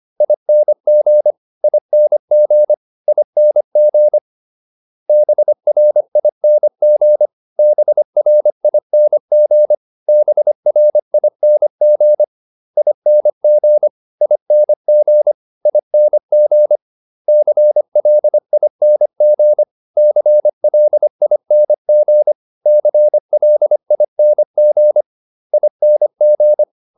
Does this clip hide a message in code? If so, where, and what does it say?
Yes: Morse code from 0.2 to 26.6 s, heard as a keyed tone, about 25 wpm, 600 Hz. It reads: ING ING ING BRING BRING BRING ING ING ING CLING CLING CLING ING